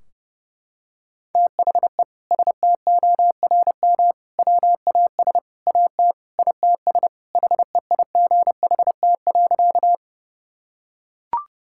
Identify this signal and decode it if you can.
Morse code, heard as a keyed tone, about 30 wpm, 700 Hz. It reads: THE STORM WAS AT ITS HEIGHT.
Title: Sentences from Top 1600 Words 30wpm